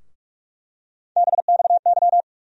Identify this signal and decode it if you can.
Morse code, heard as a keyed tone, about 44 wpm, 700 Hz. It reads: BXY
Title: USA Call Sign Postfixes - 3 Letters 45wpm